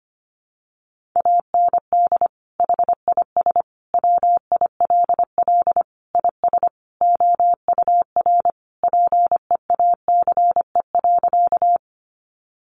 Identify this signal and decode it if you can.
Morse code, heard as a keyed tone, about 25 wpm, 700 Hz. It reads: AND HIS WILL IS OUR PEACE.
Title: Sentences from Top 1500 Words 25wpm